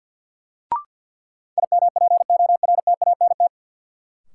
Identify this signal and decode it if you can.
Morse code, heard as a keyed tone, about 50 wpm, 700 Hz. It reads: IMPORTANT
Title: Sentences from Top 600 Words - New Word Drill - 50wpm